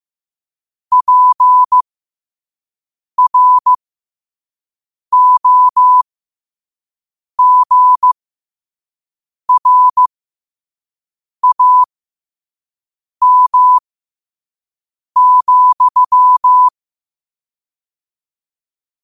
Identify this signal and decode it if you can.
Morse code, heard as a keyed tone, about 15 wpm, spaced as for 5 wpm, 1000 Hz. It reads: PROGRAM,
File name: qotdfiles / 5wpm